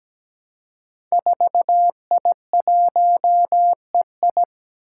Morse code, heard as a keyed tone, about 17 wpm, 700 Hz. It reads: 4I1EI
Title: Five Challenging Letter-Number Combinations 17wpm